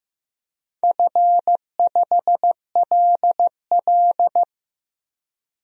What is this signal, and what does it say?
Morse code, heard as a keyed tone, about 15 wpm, 700 Hz. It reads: F5LL
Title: DXCC-100 Call Signs - 4 Characters - Repeated 2x - 15wpm